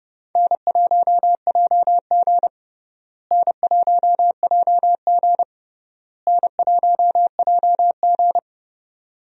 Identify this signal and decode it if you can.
Morse code, heard as a keyed tone, about 30 wpm, 700 Hz. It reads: N1JG N1JG N1JG
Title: USA Call Signs - 4 Characters - Repeated 3x - 30wpm